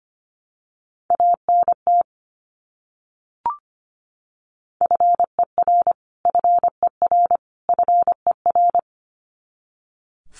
Morse code, heard as a keyed tone, about 25 wpm, 700 Hz. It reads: ANT FER FER FER
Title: Abbreviations Repeated 3x - 25wpm